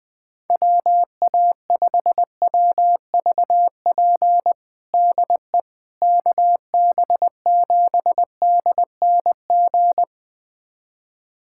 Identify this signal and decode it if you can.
Morse code, heard as a keyed tone, about 20 wpm, 700 Hz. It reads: WA5WVP DE KB7DNG
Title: USA Callsign DE Callsign - 6 Characters - 20wpm